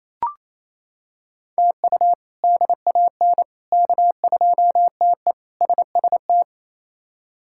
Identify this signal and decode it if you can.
Morse code, heard as a keyed tone, about 28 wpm, 700 Hz. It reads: TU DAN K2TE SST
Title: K1USN SST QSO Practice - 28wpm